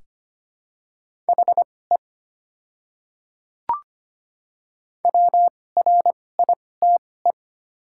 Morse code, heard as a keyed tone, about 25 wpm, spaced as for 18 wpm, 700 Hz. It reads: HE WRITE
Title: 2x Character Spacing - Top 200 Words - 25wpm